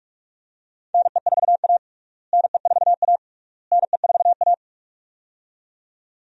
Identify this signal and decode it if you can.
Morse code, heard as a keyed tone, about 46 wpm, 700 Hz. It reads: NE4A NE4A NE4A
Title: USA Call Signs - 4 Characters - Repeated 3x - 45wpm